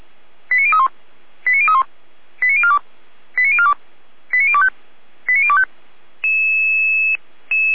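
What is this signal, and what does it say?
Five-tone selective call (ZVEI), heard as quick runs of five tones, tones of about 80 ms. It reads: 89031 89031 89042 89042 89025 89025
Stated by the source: Five-tone ZVEI selective call received by SDR, logged at audio.f85360.feuerwehr-orlamuende.org (Feuerwehr Orlamünde fire service).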